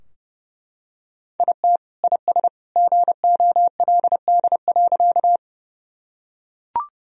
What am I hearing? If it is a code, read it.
Morse code, heard as a keyed tone, about 30 wpm, 700 Hz. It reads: IT IS GOLD.
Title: Sentences from Top 1000 Words - Review Part 3 - 30wpm